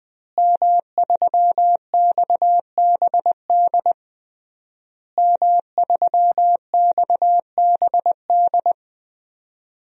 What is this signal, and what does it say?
Morse code, heard as a keyed tone, about 20 wpm, 700 Hz. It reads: M3XBD M3XBD
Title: DXCC-100 Call Signs - 5 Characters - Repeated 3x - 20wpm